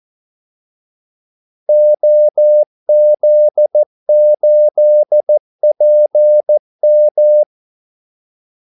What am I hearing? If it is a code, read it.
Morse code, heard as a keyed tone, about 14 wpm, 600 Hz. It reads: OZ8PM
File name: Callsigns-14wpm